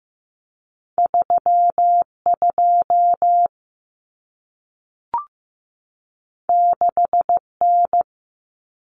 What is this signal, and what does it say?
Morse code, heard as a keyed tone, about 15 wpm, 700 Hz. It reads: 3 2 6 N
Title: Common Cut Numbers - 2-Digit 15wpm